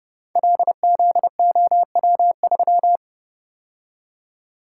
Morse code, heard as a keyed tone, about 30 wpm, 700 Hz. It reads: LZOW3
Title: Code Groups - 5 Letter:Numbers - 30wpm